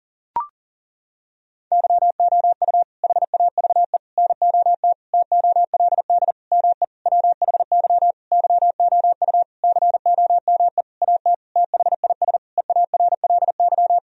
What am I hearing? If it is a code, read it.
Morse code, heard as a keyed tone, about 40 wpm, 700 Hz. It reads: YOU HAVE NOT TOLD ME WHY YOU COME AT THIS EARLY
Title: Sentences from Top 1000 Words - Review Part 2 - 40wpm